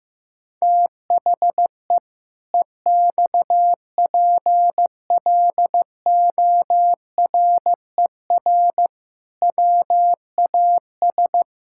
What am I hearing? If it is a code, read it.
Morse code, heard as a keyed tone, about 15 wpm, 700 Hz. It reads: THE EXPLORER WAS
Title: Pangram Challenge v2 - 15wpm